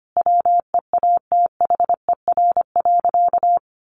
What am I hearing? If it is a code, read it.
Morse code, heard as a keyed tone, about 25 wpm, 700 Hz. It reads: WEATHER.
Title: Sentences from Top 1200 Words 25wpm